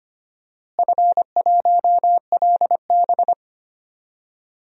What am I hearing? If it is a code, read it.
Morse code, heard as a keyed tone, about 25 wpm, 700 Hz. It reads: F1LB
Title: DXCC-100 Call Signs - 4 Characters - Repeated 2x - 25wpm